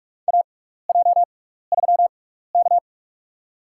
Morse code, heard as a keyed tone, about 44 wpm, spaced as for 15 wpm, 700 Hz. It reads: AJ3K
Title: USA Call Signs - 4 Characters - 6x Character Spacing - 45wpm